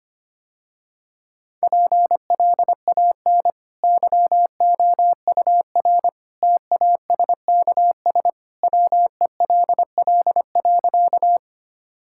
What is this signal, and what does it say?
Morse code, heard as a keyed tone, about 25 wpm, 700 Hz. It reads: PLAN YOUR TASKS WELL.
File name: Easy Sentences with Hints - 25wpm